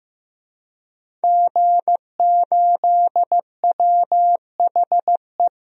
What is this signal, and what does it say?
Morse code, heard as a keyed tone, about 16 wpm, 700 Hz. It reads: G8WHE
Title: DXCC-100 Call Signs - 5 Characters - Repeated 2x - 15wpm